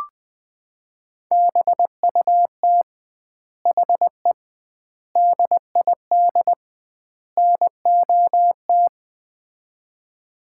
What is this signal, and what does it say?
Morse code, heard as a keyed tone, about 20 wpm, 700 Hz. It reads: BUT HE DID NOT
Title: 2x Word-Spacing - Sets of 4 Words - 20wpm